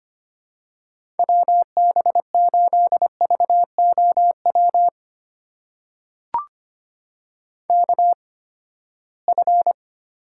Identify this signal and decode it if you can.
Morse code, heard as a keyed tone, about 25 wpm, spaced as for 17 wpm, 700 Hz. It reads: WB8VOW KF
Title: USA Call Signs - 6 Characters - 8x Character Spacing - 25wpm